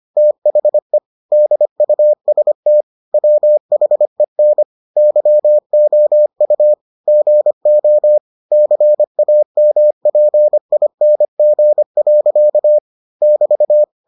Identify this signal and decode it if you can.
Morse code, heard as a keyed tone, about 25 wpm, 600 Hz. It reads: THE DUST WHEN YOU GO CAMPING. <BT>